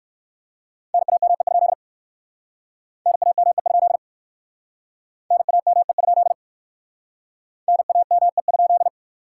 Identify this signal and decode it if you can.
Morse code, heard as a keyed tone, about 64 wpm, 700 Hz. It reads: NAME? NAME? NAME? NAME?